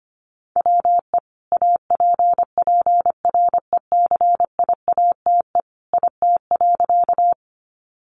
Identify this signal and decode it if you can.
Morse code, heard as a keyed tone, about 25 wpm, 700 Hz. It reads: WE APPRECIATE IT.
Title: Sentences from Top 1300 Words 25wpm